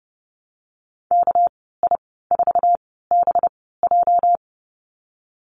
Morse code, heard as a keyed tone, about 31 wpm, spaced as for 17 wpm, 700 Hz. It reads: KI4BJ
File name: Lesson 37g - Letter x - Call Signs 30wpm